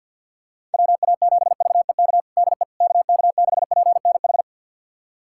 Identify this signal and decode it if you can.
Morse code, heard as a keyed tone, about 51 wpm, 700 Hz. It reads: WA7VEK DE KK6PNH
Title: USA Callsign DE Callsign - 6 Characters - 50wpm